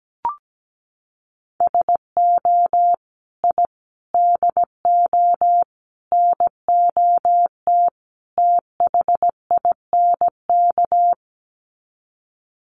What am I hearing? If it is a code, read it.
Morse code, heard as a keyed tone, about 17 wpm, 700 Hz. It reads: SO I DO NOT THINK
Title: Sets of 5 Words 17wpm